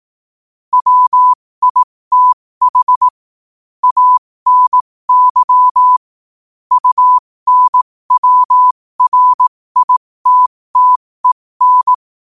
Morse code, heard as a keyed tone, about 18 wpm, spaced as for 15 wpm, 1000 Hz. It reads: WITH ANY UNWRITTEN
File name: qotdfiles / 15wpm